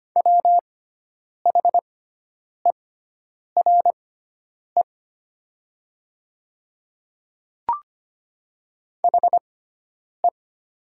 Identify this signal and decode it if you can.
Morse code, heard as a keyed tone, about 25 wpm, spaced as for 9 wpm, 700 Hz. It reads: WHERE HE
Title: 6x Character Spacing - Top 300 Words - 25wpm